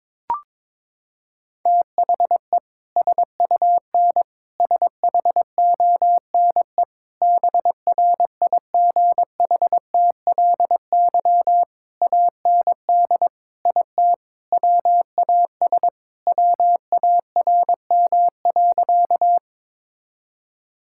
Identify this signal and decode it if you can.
Morse code, heard as a keyed tone, about 22 wpm, 700 Hz. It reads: THE SUN SHONE BRIGHTLY AND IT WAS WARM.
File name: practice / 22wpm / Sentences with 2-Word Phrases - 22wpm